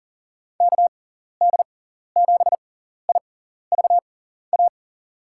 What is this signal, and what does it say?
Morse code, heard as a keyed tone, about 40 wpm, spaced as for 14 wpm, 700 Hz. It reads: KD7IVA
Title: USA Call Signs - 6x Character Spacing - 40wpm